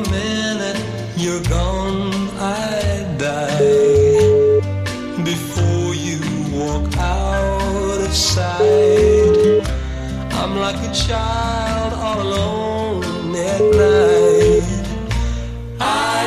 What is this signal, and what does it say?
Ringback tone (Europe, 425 Hz), with noise.